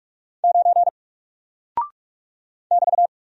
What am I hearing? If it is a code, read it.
Morse code, heard as a keyed tone, about 44 wpm, 700 Hz. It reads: 9 <BT>
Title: Single Character (Letter:Number:Punctuation:Prosign) 45wpm